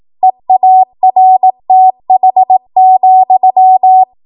Morse code, heard as a keyed tone, about 18 wpm, 750 Hz. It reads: EARTH,
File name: Morse / 18WPM